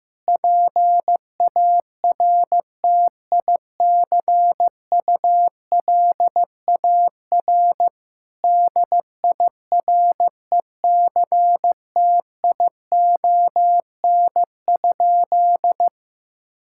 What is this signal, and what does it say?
Morse code, heard as a keyed tone, about 15 wpm, 700 Hz. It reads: PARTICULAR DIRECTION?
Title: Sentences from Top 1000 Words - Review Part 5 - 15wpm